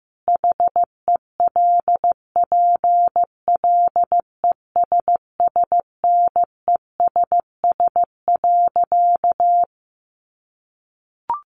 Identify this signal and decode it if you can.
Morse code, heard as a keyed tone, about 15 wpm, 700 Hz. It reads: HELPLESSNESS.
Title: A Sound Of Thunder 15wpm